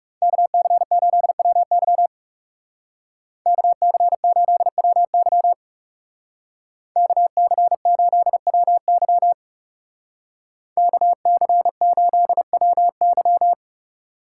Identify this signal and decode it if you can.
Morse code, heard as a keyed tone, about 36 wpm, 700 Hz. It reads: KC8WY KC8WY KC8WY KC8WY
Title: Speed Racing - USA 2x2 Call Signs 30wpm